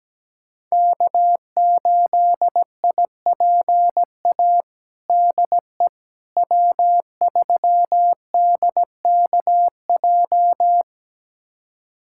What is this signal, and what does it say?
Morse code, heard as a keyed tone, about 17 wpm, 700 Hz. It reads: K8IPA DE W3DKJ
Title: USA Callsign DE Callsign - 5 Characters - 17wpm